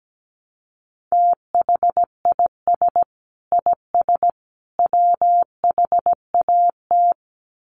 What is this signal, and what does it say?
Morse code, heard as a keyed tone, about 17 wpm, 700 Hz. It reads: THIS IS WHAT
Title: Sets of 3 Words 17wpm